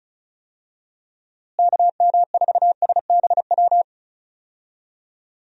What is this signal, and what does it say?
Morse code, heard as a keyed tone, about 34 wpm, 700 Hz. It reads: KM4SBW
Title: USA Call Signs - 6 Characters - Repeated 3x - 35wpm